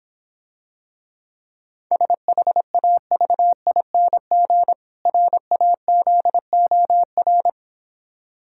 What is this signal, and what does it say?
Morse code, heard as a keyed tone, about 26 wpm, 700 Hz. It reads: SHAVING RAZOR